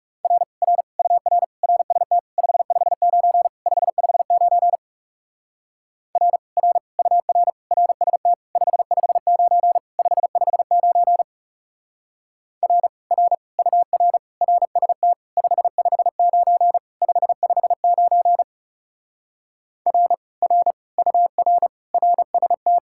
Morse code, heard as a keyed tone, about 38 wpm, 700 Hz. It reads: R R UR RST 559 559 R R UR RST 559 559 R R UR RST 559 559 R R UR RST